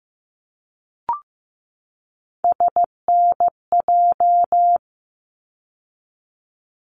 Morse code, heard as a keyed tone, about 15 wpm, 700 Hz. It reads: SNJ
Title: ARRL Sections 15wpm